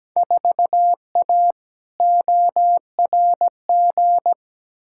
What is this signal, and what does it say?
Morse code, heard as a keyed tone, about 17 wpm, 700 Hz. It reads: 4A ORG
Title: Field Day Exchanges - 17wpm